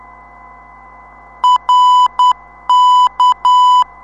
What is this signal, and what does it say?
Morse code, heard as a keyed tone, about 10 wpm, 1000 Hz. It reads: RK
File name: ndb zvok / rk 359khz 25mar2016 1447utc